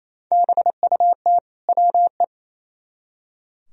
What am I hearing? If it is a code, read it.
Morse code, heard as a keyed tone, about 34 wpm, 700 Hz. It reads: BUT WE
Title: Speed-Racing - Sets of 2 Words 28wpm